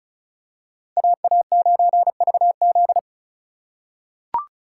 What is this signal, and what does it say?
Morse code, heard as a keyed tone, about 34 wpm, 690 Hz. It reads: AA9VZ
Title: USA Call Signs - 3x Character Spacing - 35wpm